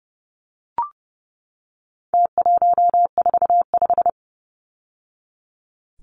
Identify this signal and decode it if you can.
Morse code, heard as a keyed tone, about 30 wpm, 700 Hz. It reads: T145
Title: Numbers - Contest Style - 4-Digit 30wpm